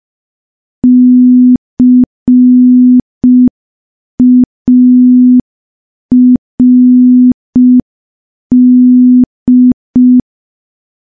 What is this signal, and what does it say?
Morse code, heard as a keyed tone, about 5 wpm, 250 Hz. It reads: CARD
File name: morse / morse-training-05WPM